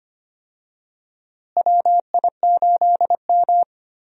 Morse code, heard as a keyed tone, about 25 wpm, 700 Hz. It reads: WI8M
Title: USA Call Signs - 4 Characters - Repeated 2x - 25wpm